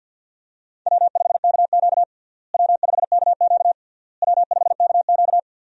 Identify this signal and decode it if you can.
Morse code, heard as a keyed tone, about 47 wpm, 700 Hz. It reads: W5KQ W5KQ W5KQ
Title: USA Call Signs - 4 Characters - Repeated 3x - 50wpm